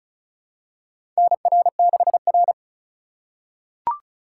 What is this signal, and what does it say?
Morse code, heard as a keyed tone, about 36 wpm, 700 Hz. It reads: NR6R
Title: USA Call Signs - 4x Character Spacing - 35wpm